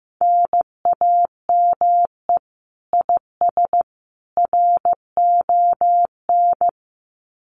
Morse code, heard as a keyed tone, about 15 wpm, 700 Hz. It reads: NAME IS RON
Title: Common QSO Phrases 15wpm